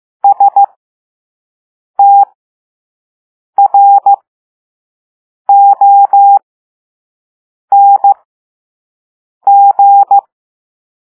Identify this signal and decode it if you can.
Morse code, heard as a keyed tone, about 15 wpm, spaced as for 5 wpm, 800 Hz. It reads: STRONG
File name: Morse / Level5-All-B